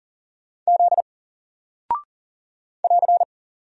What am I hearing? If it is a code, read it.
Morse code, heard as a keyed tone, about 40 wpm, 700 Hz. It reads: Z <AR>
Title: Single Character (Letter:Number:Punctuation:Prosign) 40wpm